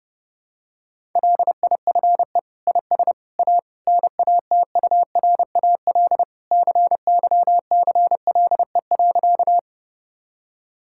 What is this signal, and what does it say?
Morse code, heard as a keyed tone, about 32 wpm, 700 Hz. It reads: LIFE IS A NATURAL CYCLE.
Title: Sentences from Top 1000 Words - Review Part 6 - 30wpm